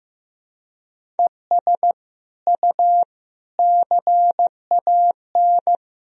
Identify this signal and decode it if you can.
Morse code, heard as a keyed tone, about 15 wpm, 700 Hz. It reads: ES U CAN
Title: Sentences from Top 400 Words and Abbreviations 15wpm